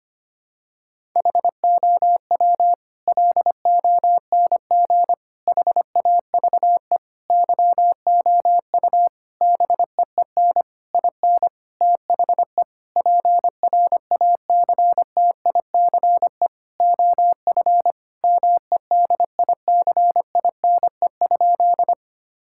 Morse code, heard as a keyed tone, about 25 wpm, 700 Hz. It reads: HOW LONG HAVE YOU BEEN IN THE PRACTICE OF MEDICINE?
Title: Sentences from Top 1000 Words - Review Part 6 - 25wpm